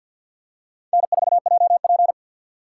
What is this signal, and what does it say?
Morse code, heard as a keyed tone, about 52 wpm, 700 Hz. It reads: N4JP
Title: USA Call Signs - 4 Characters - 2x Character Spacing - 50wpm